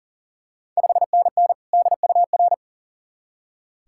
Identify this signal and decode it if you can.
Morse code, heard as a keyed tone, about 40 wpm, 690 Hz. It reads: 5NN DUR
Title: Ontario QSO Party Exchange 40wpm